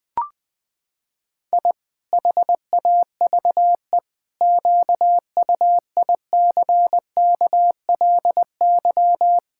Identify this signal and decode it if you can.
Morse code, heard as a keyed tone, about 20 wpm, 700 Hz. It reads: I HAVE QUICKLY
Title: Pangram Challenge v2 - 20wpm